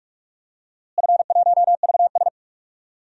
Morse code, heard as a keyed tone, about 46 wpm, 700 Hz. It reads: F1VS
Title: DXCC-100 Call Signs - 4 Characters - Repeated 2x - 45wpm